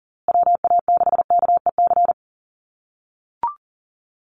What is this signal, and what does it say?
Morse code, heard as a keyed tone, about 40 wpm, 700 Hz. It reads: WA6KEC
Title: USA Call Signs Repeated 2x 40wpm